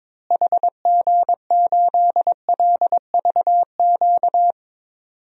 Morse code, heard as a keyed tone, about 22 wpm, 700 Hz. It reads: HG8LVQ
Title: DXCC-100 Call Signs - 6 Characters - Repeated 3x - 22wpm